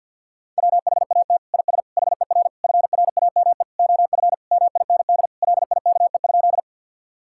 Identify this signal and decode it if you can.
Morse code, heard as a keyed tone, about 50 wpm, 700 Hz. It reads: WHAT IS HER FRAME OF MIND LIKE?
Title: Sentences from Top 1000 Words - Review Part 6 - 50wpm